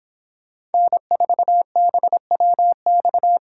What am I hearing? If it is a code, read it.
Morse code, heard as a keyed tone, about 31 wpm, 700 Hz. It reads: N4BWX